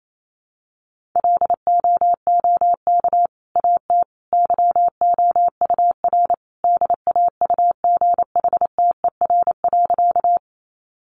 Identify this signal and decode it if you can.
Morse code, heard as a keyed tone, about 28 wpm, 700 Hz. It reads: LOOK AT YOUR DAUGHTER.
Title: Sentences from Top 1500 Words 28wpm